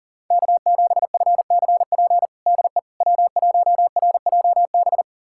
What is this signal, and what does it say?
Morse code, heard as a keyed tone, about 40 wpm, 700 Hz. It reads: K7FCP DE W1RJB
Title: USA Callsign DE Callsign - 5 Characters - 40wpm